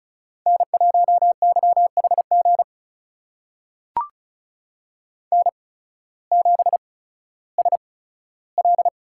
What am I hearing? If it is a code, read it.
Morse code, heard as a keyed tone, about 36 wpm, spaced as for 16 wpm, 700 Hz. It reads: N1YHG N7SL
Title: USA Call Signs - 5 Characters - 8x Character Spacing - 35wpm